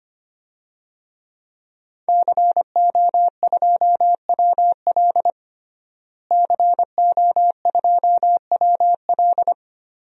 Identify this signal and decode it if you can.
Morse code, heard as a keyed tone, about 25 wpm, 700 Hz. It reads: CO2WL CO2WL
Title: DXCC-100 Call Signs - 5 Characters - Repeated 3x - 25wpm